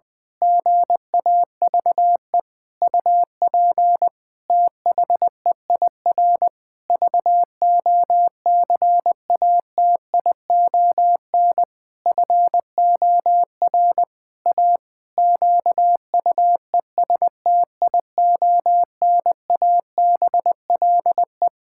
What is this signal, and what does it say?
Morse code, heard as a keyed tone, about 20 wpm, 700 Hz. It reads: GAVE UP THEIR VOCATION FOR A QUESTIONABLE
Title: Pangram Challenge v2 - 20wpm